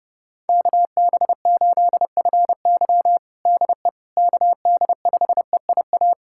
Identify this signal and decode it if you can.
Morse code, heard as a keyed tone, about 30 wpm, 700 Hz. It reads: KB8FY DE KD5EIA
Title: USA Callsign DE Callsign 30wpm